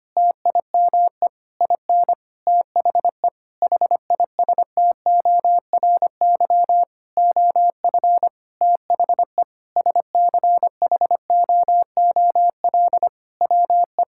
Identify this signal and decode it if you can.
Morse code, heard as a keyed tone, about 25 wpm, 700 Hz. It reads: TIME IN THE HISTORY OF THE SCHOOL WE
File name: practice / 25wpm / Sentences with 9-Word Phrases - 25wpm